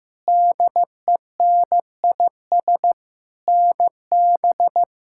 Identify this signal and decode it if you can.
Morse code, heard as a keyed tone, about 15 wpm, 700 Hz. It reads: DENIS NB
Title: K1USN SST QSO Practice - 15wpm